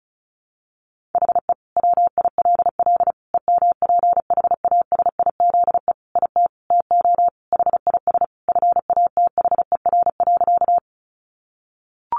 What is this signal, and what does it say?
Morse code, heard as a keyed tone, about 34 wpm, 700 Hz. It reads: HE WILL EMPHASIZE IT TO HIS FATHER.
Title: Sentences from Top 1500 Words 35wpm